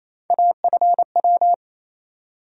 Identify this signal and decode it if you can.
Morse code, heard as a keyed tone, about 28 wpm, 700 Hz. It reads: AFW
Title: USA Call Sign Postfixes - 3 Letters 28wpm